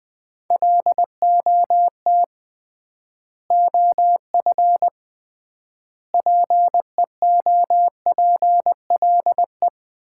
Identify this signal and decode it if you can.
Morse code, heard as a keyed tone, about 20 wpm, 700 Hz. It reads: LOT OF PEOPLE
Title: 3x Word-Spacing - Sets of 3 Words - 20wpm